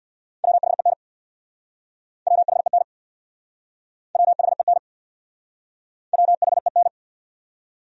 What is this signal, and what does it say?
Morse code, heard as a keyed tone, about 60 wpm, 690 Hz. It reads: WHEN WHEN WHEN WHEN
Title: Speed-Racing Top 100 Words 50wpm